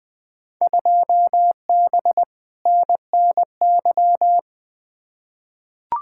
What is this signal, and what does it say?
Morse code, heard as a keyed tone, about 21 wpm, 700 Hz. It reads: 2B NNY
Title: Field Day Exchanges Encore - 20wpm